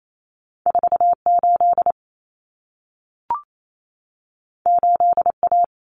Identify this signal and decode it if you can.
Morse code, heard as a keyed tone, about 28 wpm, 700 Hz. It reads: 48 8A
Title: Common Cut Numbers - 2-Digit 28wpm